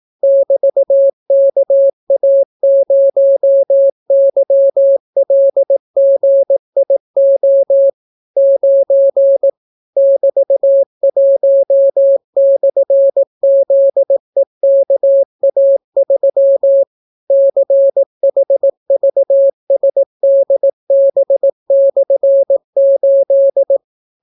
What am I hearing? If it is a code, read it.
Morse code, heard as a keyed tone, about 18 wpm, 550 Hz. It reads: <BT>KA0YLGIO 9 <BT>1/ZEKA3 CHVSDB/8